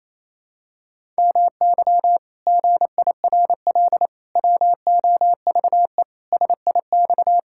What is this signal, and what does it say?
Morse code, heard as a keyed tone, about 28 wpm, 700 Hz. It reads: MY GIRL WOVE SIX
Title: Pangram Challenge v2 - 28wpm